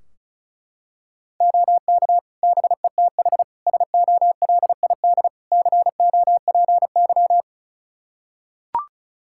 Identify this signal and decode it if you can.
Morse code, heard as a keyed tone, about 36 wpm, 700 Hz. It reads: OK BETH SOLID COPY
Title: Practice QSO - Exchange 3 - More Info - 35wpm